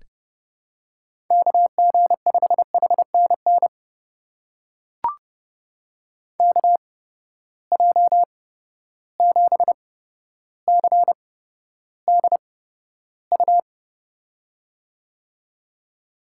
Morse code, heard as a keyed tone, about 30 wpm, spaced as for 13 wpm, 700 Hz. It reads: KG5HNN KJ7CDU
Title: USA Call Signs - 6 Characters - 8x Character Spacing - 30wpm